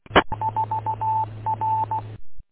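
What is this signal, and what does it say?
Morse code, heard as a keyed tone, about 16 wpm, 880 Hz. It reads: 4 R
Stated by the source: Morse code recorded in the W8UM 145230 audio archive, on 145.230.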